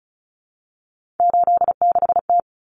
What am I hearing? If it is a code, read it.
Morse code, heard as a keyed tone, about 34 wpm, 700 Hz. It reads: 86T
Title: Common Cut Numbers - 3-Digit 35wpm